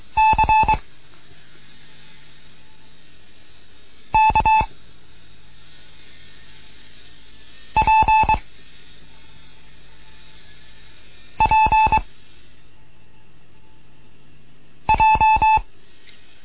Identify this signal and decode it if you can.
Morse code, heard as a keyed tone, about 22 wpm, 870 Hz. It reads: C K P P J